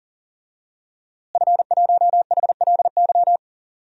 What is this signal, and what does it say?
Morse code, heard as a keyed tone, about 40 wpm, 700 Hz. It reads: F1HLY
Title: DXCC-100 Call Signs - 5 Characters - Repeated 3x - 40wpm